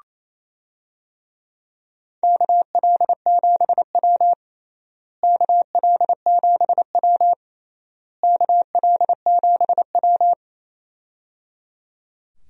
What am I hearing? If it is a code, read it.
Morse code, heard as a keyed tone, about 28 wpm, 700 Hz. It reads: KL7W KL7W KL7W
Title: DXCC-100 Call Signs - 4 Characters - Repeated 3x - 28wpm